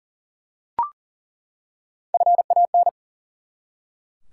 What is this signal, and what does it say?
Morse code, heard as a keyed tone, about 40 wpm, 700 Hz. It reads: FAN
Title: Three Letter Words - 40wpm